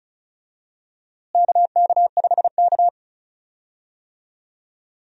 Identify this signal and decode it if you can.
Morse code, heard as a keyed tone, about 36 wpm, 700 Hz. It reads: KK5K
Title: USA Call Signs - 4 Characters - Repeated 2x - 35wpm